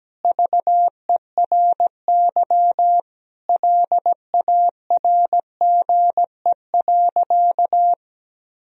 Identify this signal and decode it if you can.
Morse code, heard as a keyed tone, about 17 wpm, 700 Hz. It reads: VERY LARGE.
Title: Sentences from Top 1000 Words - Review Part 1 - 17wpm